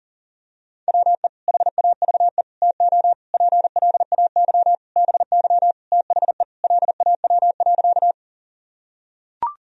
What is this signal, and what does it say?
Morse code, heard as a keyed tone, about 37 wpm, 700 Hz. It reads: WE HAVE TO PLAY BY THE LAW.
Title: Sentences from Top 1000 Words - Review Part 3 - 40wpm